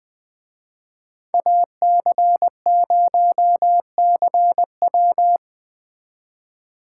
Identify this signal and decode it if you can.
Morse code, heard as a keyed tone, about 20 wpm, 700 Hz. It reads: AC0CW
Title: Speed Racing - USA 2x2 Call Signs 17wpm